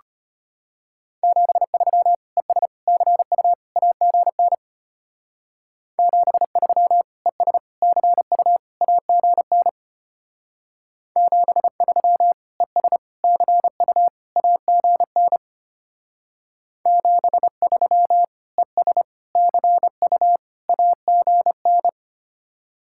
Morse code, heard as a keyed tone, about 31 wpm, 700 Hz. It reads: 73 ES CU AGN 73 ES CU AGN 73 ES CU AGN 73 ES CU AGN
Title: Speed-Racing Common QSO Phrases - 25wpm